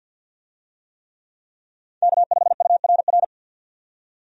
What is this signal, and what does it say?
Morse code, heard as a keyed tone, about 52 wpm, 700 Hz. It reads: K5URR
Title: USA Call Signs - 5 Characters - 50wpm